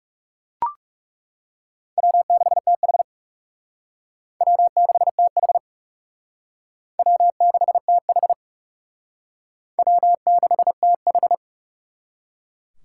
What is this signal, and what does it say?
Morse code, heard as a keyed tone, about 39 wpm, 700 Hz. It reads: W6TH W6TH W6TH W6TH